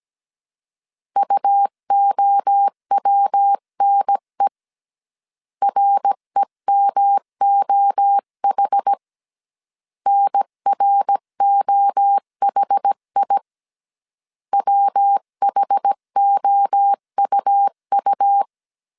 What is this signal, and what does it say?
Morse code, heard as a keyed tone, about 17 wpm, 800 Hz. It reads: UOWNE REMOH NROHI WHOUU
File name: CW / DET / 1 3 1500C S A 16wpm 2141